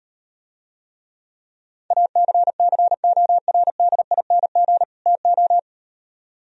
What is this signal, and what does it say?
Morse code, heard as a keyed tone, about 38 wpm, 700 Hz. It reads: ACCORDING TO